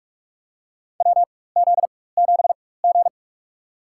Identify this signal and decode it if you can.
Morse code, heard as a keyed tone, about 44 wpm, spaced as for 21 wpm, 700 Hz. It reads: WZ7G
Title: USA Call Signs - 4 Characters - 4x Character Spacing - 45wpm